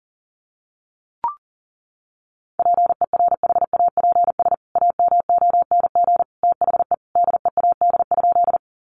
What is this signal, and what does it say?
Morse code, heard as a keyed tone, about 40 wpm, 700 Hz. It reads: PERHAPS AMONG THE DEAD?
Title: Sentences from Top 1500 Words 40wpm